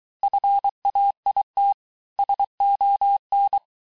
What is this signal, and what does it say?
Morse code, heard as a keyed tone, about 23 wpm, 800 Hz. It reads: FAIT SON
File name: cours-du-lundi files / cours-1852-texte-clair-vitesse-17MPM-puis-22MPM